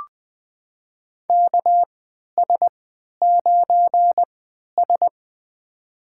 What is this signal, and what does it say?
Morse code, heard as a keyed tone, about 20 wpm, spaced as for 11 wpm, 700 Hz. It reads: KS9S
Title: USA Call Signs - 4 Characters - 3x Character Spacing - 20wpm